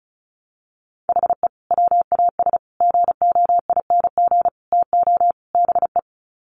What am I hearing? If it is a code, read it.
Morse code, heard as a keyed tone, about 34 wpm, 700 Hz. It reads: HE WAS GOING TO BE
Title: Sets of 5 Words 35wpm